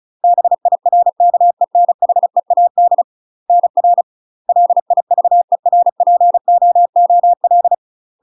Morse code, heard as a keyed tone, about 36 wpm, 700 Hz. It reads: BIRKENHEAD NR LIVERPOOL
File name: UK MASTER QSOs TRAINING / 35WPM / UK MASTER QSOs 0018